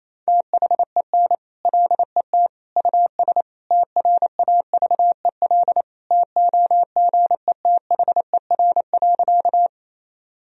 Morse code, heard as a keyed tone, about 28 wpm, 700 Hz. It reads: THEN LET US TRAVEL TOGETHER.
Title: Sentences from Top 1000 Words - Review Part 2 - 28wpm